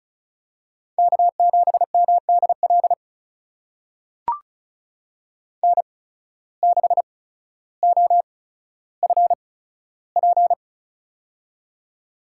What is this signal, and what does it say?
Morse code, heard as a keyed tone, about 36 wpm, spaced as for 15 wpm, 700 Hz. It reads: K7MDL N6OFP
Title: USA Call Signs - 8x Character Spacing - 35wpm